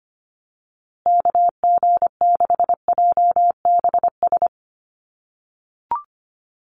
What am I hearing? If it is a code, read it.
Morse code, heard as a keyed tone, about 25 wpm, 700 Hz. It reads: KG6JBS
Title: USA Call Signs Repeated 3x 25wpm